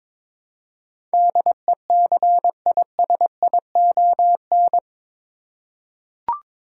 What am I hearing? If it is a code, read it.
Morse code, heard as a keyed tone, about 22 wpm, 700 Hz. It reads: DECISION
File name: practice / 22wpm / Eight Letter Words - 22wpm